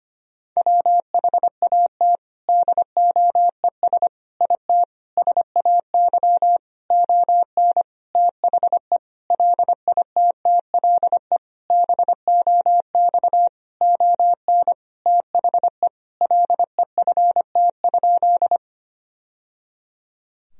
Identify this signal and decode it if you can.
Morse code, heard as a keyed tone, about 25 wpm, 700 Hz. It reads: WHAT DOES IT SAY ON THE LITTLE BOX ON THE LEFT?
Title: Sentences from Top 1000 Words - Review Part 2 - 25wpm